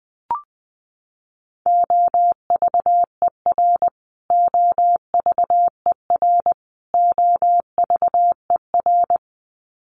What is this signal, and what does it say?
Morse code, heard as a keyed tone, about 20 wpm, 700 Hz. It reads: OVER OVER OVER
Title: Top 500 Words Repeated 20wpm